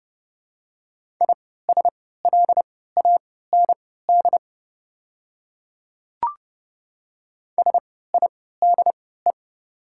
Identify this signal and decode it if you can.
Morse code, heard as a keyed tone, about 29 wpm, spaced as for 17 wpm, 700 Hz. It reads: ISLAND SIDE
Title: 3x Character Spacing - Top 500 Words - 30wpm